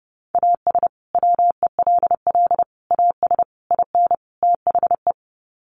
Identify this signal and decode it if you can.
Morse code, heard as a keyed tone, about 30 wpm, 700 Hz. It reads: AS WELL AS IN THE
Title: Sets of 5 Words Encore 30wpm